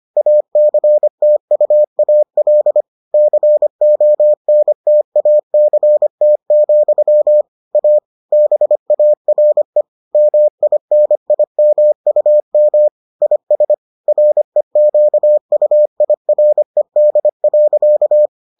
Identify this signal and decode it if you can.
Morse code, heard as a keyed tone, about 25 wpm, 600 Hz. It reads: ACTUAL CONTACT, A BARE MINIMUM IS REQUIRED.